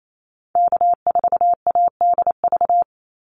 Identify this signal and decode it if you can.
Morse code, heard as a keyed tone, about 28 wpm, 700 Hz. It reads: K4ADV
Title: NA SOTA Activator Callsigns 28wpm